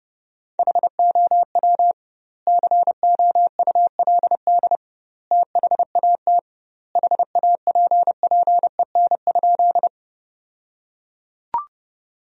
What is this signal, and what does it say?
Morse code, heard as a keyed tone, about 32 wpm, 700 Hz. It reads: HOW COULD THAT HAPPEN?
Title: 2x Word-Spacing - Sentences from Top 400 Words - 30wpm